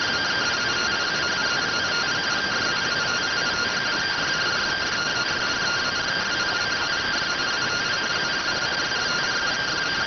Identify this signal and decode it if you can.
Two-tone teleprinter signal, about 170 Hz shift.